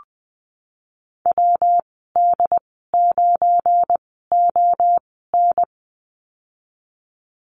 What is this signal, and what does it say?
Morse code, heard as a keyed tone, about 19 wpm, spaced as for 14 wpm, 700 Hz. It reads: WD9ON
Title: 179 Lesson 29g - Slash - Call Signs 20wpm